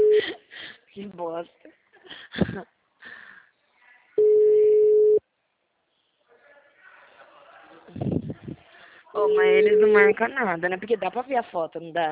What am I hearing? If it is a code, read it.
Ringback tone (Europe, 425 Hz), with speech.
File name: out- -2004- - - .165